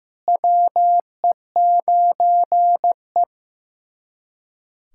Morse code, heard as a keyed tone, about 15 wpm, 700 Hz. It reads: WE9E
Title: USA Callsign DE Callsign - 4 Characters - 15wpm